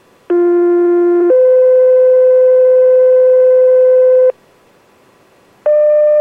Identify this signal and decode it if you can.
Two-tone page, heard as a one-second tone, then a three-second tone, about 350 then 510 Hz. It is heard from 0.3 to 4.3 s.